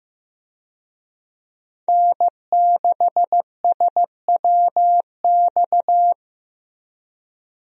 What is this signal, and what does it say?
Morse code, heard as a keyed tone, about 15 wpm, 700 Hz. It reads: N6SWX
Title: USA Call Signs - 5 Characters - Repeated 2x - 15wpm